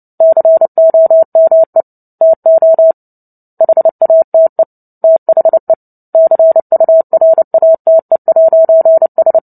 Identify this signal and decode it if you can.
Morse code, heard as a keyed tone, about 30 wpm, 650 Hz. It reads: COME TO HATE THE CURATE'S
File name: Book Two / 03-Chapter 3 - 29 WPM